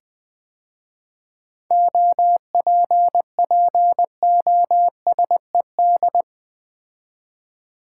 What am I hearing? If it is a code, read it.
Morse code, heard as a keyed tone, about 20 wpm, 700 Hz. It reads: OPPOSED
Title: 7 Letter Words - First 3 Letters Given - 20wpm